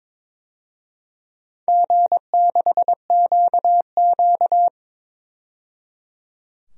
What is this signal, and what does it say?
Morse code, heard as a keyed tone, about 22 wpm, 700 Hz. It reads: G6QQ
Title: DXCC-100 Call Signs - 4 Characters - 22wpm